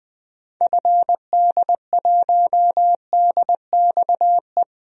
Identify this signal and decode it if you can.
Morse code, heard as a keyed tone, about 20 wpm, 700 Hz. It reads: FD1DXE
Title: DXCC-100 Call Signs - 6 Characters - Repeated 3x - 20wpm